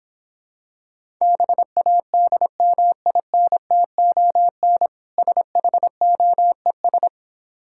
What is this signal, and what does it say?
Morse code, heard as a keyed tone, about 26 wpm, 700 Hz. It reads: BADMINTON SHOES